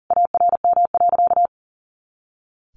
Morse code, heard as a keyed tone, about 40 wpm, 700 Hz. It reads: ARM.
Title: A Sound Of Thunder 40wpm